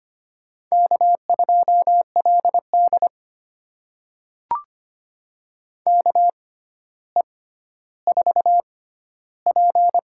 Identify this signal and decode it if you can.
Morse code, heard as a keyed tone, about 25 wpm, spaced as for 13 wpm, 700 Hz. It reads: K2LD KE4P
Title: USA Call Signs - 4 Characters - 6x Character Spacing - 25wpm